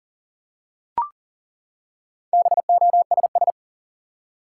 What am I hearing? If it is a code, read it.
Morse code, heard as a keyed tone, about 40 wpm, 700 Hz. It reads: BOSS
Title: Four Letter Words - 40wpm